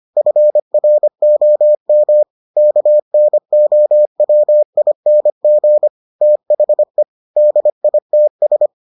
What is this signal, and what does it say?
Morse code, heard as a keyed tone, about 25 wpm, 600 Hz. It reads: FROM KNOWING THE DITS